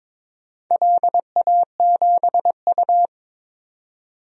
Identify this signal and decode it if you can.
Morse code, heard as a keyed tone, about 22 wpm, 700 Hz. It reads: LA7U
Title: DXCC-100 Call Signs - 4 Characters - 22wpm